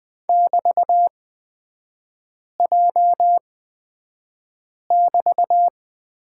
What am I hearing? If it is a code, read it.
Morse code, heard as a keyed tone, about 20 wpm, spaced as for 5 wpm, 700 Hz. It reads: <BT>J<BT>
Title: 224 Lesson 36e - BT - Three Characters 20wpm